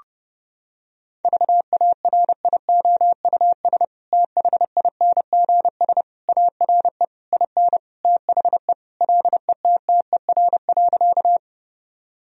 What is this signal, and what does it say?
Morse code, heard as a keyed tone, about 32 wpm, 700 Hz. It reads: VARIOUS THINGS ARE IN THE LETTER.
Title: Sentences from Top 1000 Words - Review Part 4 - 30wpm